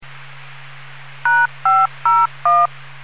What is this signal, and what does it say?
DTMF, heard as telephone touch tones, tones of about 220 ms. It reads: #501